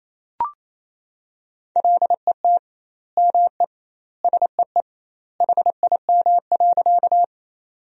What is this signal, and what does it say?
Morse code, heard as a keyed tone, about 28 wpm, 700 Hz. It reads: LET ME SEE HIM.
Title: 2x Word-Spacing - Sentences from Top 300 Words - 28wpm